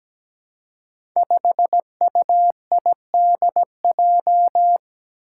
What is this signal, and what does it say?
Morse code, heard as a keyed tone, about 17 wpm, 700 Hz. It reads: 5UIDJ
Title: Five Challenging Letter-Number Combinations 17wpm